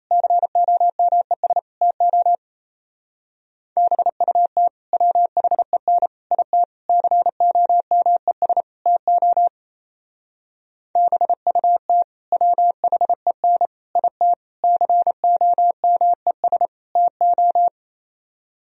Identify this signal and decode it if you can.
Morse code, heard as a keyed tone, about 32 wpm, 700 Hz. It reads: COMES TO BUT WHEN IT COMES TO BUT WHEN IT COMES TO
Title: Speed-Racing - Sets of 5 Words 28wpm